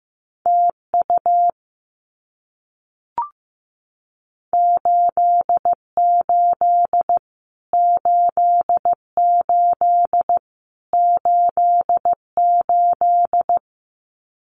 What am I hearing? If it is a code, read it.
Morse code, heard as a keyed tone, about 15 wpm, 700 Hz. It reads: TU 88 88 88
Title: Abbreviations Repeated 3x - 15wpm